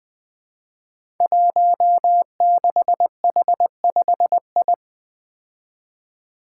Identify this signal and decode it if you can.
Morse code, heard as a keyed tone, about 20 wpm, 700 Hz. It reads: 16H5I
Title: Five Challenging Letter-Number Combinations 20wpm